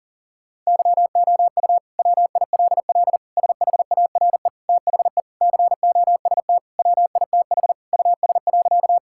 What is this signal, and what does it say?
Morse code, heard as a keyed tone, about 40 wpm, 700 Hz. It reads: YOU WILL SHARE THE COST WITH US.
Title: Sentences from Top 1000 Words - Review Part 6 - 40wpm